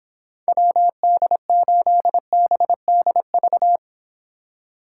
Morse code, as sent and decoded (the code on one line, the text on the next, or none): .-- -.. ---.. -... -.. ...-
WD8BDV